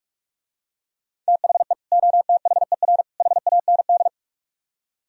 Morse code, heard as a keyed tone, about 44 wpm, 700 Hz. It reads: THE OTHER HAND